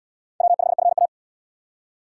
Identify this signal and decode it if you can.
Morse code, heard as a keyed tone, about 68 wpm, 700 Hz. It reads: W5FA